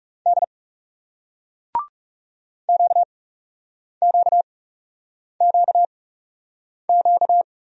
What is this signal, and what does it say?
Morse code, heard as a keyed tone, about 41 wpm, spaced as for 8 wpm, 700 Hz. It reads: D QQQQ